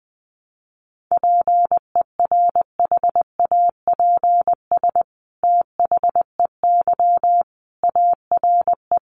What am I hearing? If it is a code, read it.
Morse code, heard as a keyed tone, about 20 wpm, 700 Hz. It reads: PERHAPS THEY ARE
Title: Sentences from Top 700 Words 20wpm